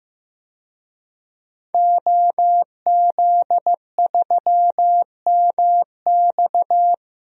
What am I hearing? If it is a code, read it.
Morse code, heard as a keyed tone, about 15 wpm, 700 Hz. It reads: OZ3MX
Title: DXCC-100 Call Signs - 5 Characters - 15wpm